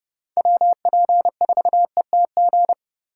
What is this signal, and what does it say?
Morse code, heard as a keyed tone, about 31 wpm, 700 Hz. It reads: WP4ETG